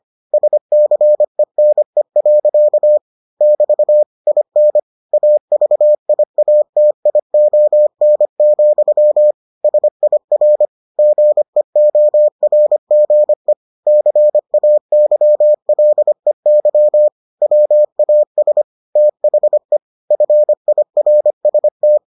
Morse code, heard as a keyed tone, about 25 wpm, 600 Hz. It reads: SCENE. <BT> IN AVIATION, SIR GEORGE CAYLEY WAS THE FIRST